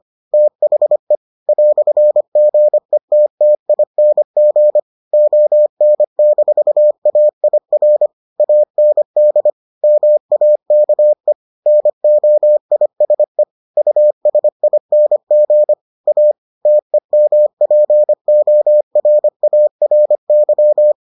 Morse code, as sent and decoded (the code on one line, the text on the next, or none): - .... / .-..-. --. - - .. -. --. / --- -. -....- .- .. .-. / .- -. -.. / -- .- -.- . / -. --- .. ... / ..- ... .. -. --. / .- / - . -- .--. --- .-. .- .-. -.--
THE "GETTING ON-AIR AND MAKE NOISE USING A TEMPORARY